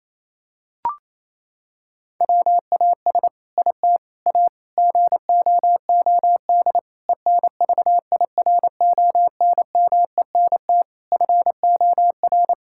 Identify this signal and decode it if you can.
Morse code, heard as a keyed tone, about 28 wpm, 700 Hz. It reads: WAS IT A GOOD ENVIRONMENT FOR
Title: Sentences from Top 1000 Words - Review Part 5 - 28wpm